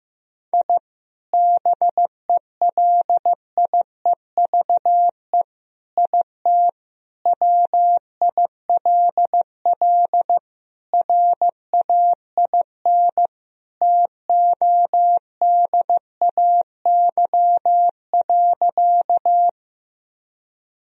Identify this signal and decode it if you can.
Morse code, heard as a keyed tone, about 15 wpm, 700 Hz. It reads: I BELIEVE IT WILL RAIN TODAY.
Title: Sentences from Top 1000 Words - Review Part 1 - 15wpm